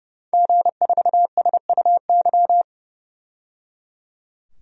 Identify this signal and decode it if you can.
Morse code, heard as a keyed tone, about 30 wpm, 700 Hz. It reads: G4SUY
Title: DXCC-100 Call Signs - 5 Characters - 30wpm